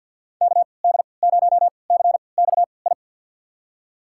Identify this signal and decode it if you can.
Morse code, heard as a keyed tone, about 52 wpm, spaced as for 29 wpm, 690 Hz. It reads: KD0CXI
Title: USA Call Signs - 6 Characters - 3x Character Spacing - 50wpm